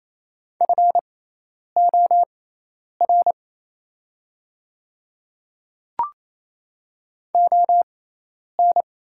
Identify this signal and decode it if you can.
Morse code, heard as a keyed tone, about 27 wpm, spaced as for 10 wpm, 700 Hz. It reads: FOR ON
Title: 6x Character Spacing - Top 100 Words - 28wpm